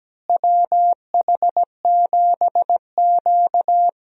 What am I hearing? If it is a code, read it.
Morse code, heard as a keyed tone, about 17 wpm, 700 Hz. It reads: WH7Q
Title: USA Call Signs - 4 Characters - Repeated 3x - 17wpm